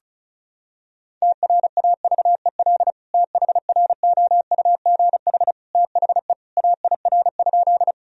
Morse code, heard as a keyed tone, about 36 wpm, 700 Hz. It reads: TRAVEL THROUGH THE AIR?
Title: Sentences from Top 1000 Words - Review Part 2 - 35wpm